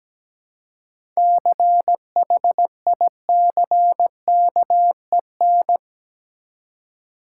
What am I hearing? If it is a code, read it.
Morse code, heard as a keyed tone, about 17 wpm, 700 Hz. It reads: CHICKEN
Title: Seven Letter Words - 17wpm